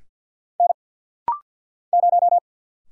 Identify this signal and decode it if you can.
Morse code, heard as a keyed tone, about 49 wpm, 700 Hz. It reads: N 0
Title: Single Letter-Number - Rapid-Fire 50wpm